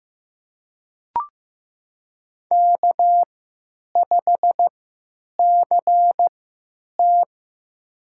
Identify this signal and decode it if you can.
Morse code, heard as a keyed tone, about 15 wpm, spaced as for 9 wpm, 700 Hz. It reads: K5CT
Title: USA Call Signs - 3x Character Spacing - 15wpm